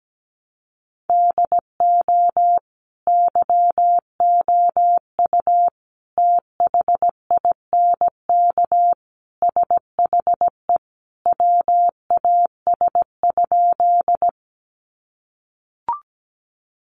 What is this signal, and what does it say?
Morse code, heard as a keyed tone, about 17 wpm, 700 Hz. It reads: DO YOU THINK SHE WAS?
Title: Sentences from Top 200 Words 17wpm